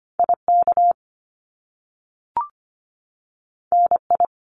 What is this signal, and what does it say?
Morse code, heard as a keyed tone, about 25 wpm, 700 Hz. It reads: IK NI
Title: Top Two Letter Combinations in Words 25wpm